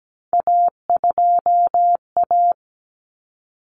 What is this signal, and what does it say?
Morse code, heard as a keyed tone, about 17 wpm, 700 Hz. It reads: A2A
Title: Common Cut Numbers - 3-Digit 17wpm